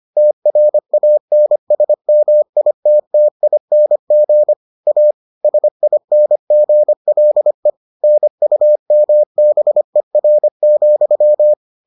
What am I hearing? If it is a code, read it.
Morse code, heard as a keyed tone, about 25 wpm, 600 Hz. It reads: TRANSMITTING A SINGLE NUMBER,